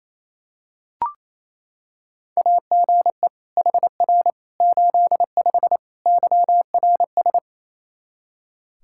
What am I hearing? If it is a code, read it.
Morse code, heard as a keyed tone, about 28 wpm, 700 Hz. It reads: AGE HR 85 YRS
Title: Practice QSO - Exchange 3 - More Info - 28wpm